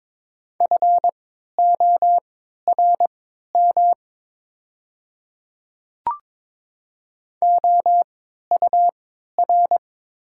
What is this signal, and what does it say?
Morse code, heard as a keyed tone, about 22 wpm, spaced as for 12 wpm, 700 Hz. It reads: FORM OUR
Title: 3x Character Spacing - Top 200 Words - 22wpm